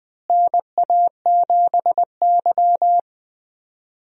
Morse code, heard as a keyed tone, about 20 wpm, 700 Hz. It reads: NA7Y
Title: USA Call Signs - 4 Characters - Repeated 3x - 20wpm